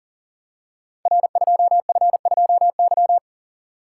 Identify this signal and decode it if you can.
Morse code, heard as a keyed tone, about 43 wpm, 700 Hz. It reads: R2F2Y
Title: Code Groups - 5 Letter:Numbers - 40wpm